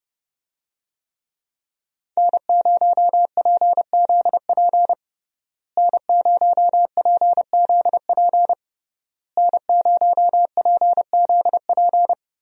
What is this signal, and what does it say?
Morse code, heard as a keyed tone, about 31 wpm, 700 Hz. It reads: N0PZP N0PZP N0PZP
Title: USA Call Signs - 5 Characters - Repeated 3x - 30wpm